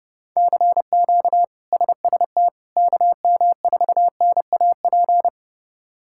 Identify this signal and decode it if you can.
Morse code, heard as a keyed tone, about 32 wpm, 700 Hz. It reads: CQ SST KM4NAP
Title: K1USN SST QSO Practice - 30wpm